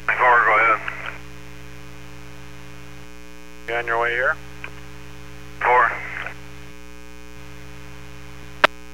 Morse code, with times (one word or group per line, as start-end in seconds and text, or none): none